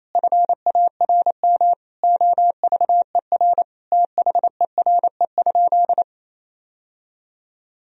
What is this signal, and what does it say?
Morse code, heard as a keyed tone, about 28 wpm, 700 Hz. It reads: FARM OVER THERE?
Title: Sentences from Top 1000 Words - Review Part 2 - 28wpm